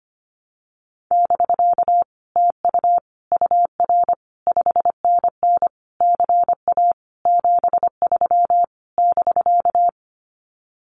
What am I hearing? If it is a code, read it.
Morse code, heard as a keyed tone, about 25 wpm, 700 Hz. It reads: <BK> TU UR 5NN CA 73 <BK>
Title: Common QSO Phrases Encore 25wpm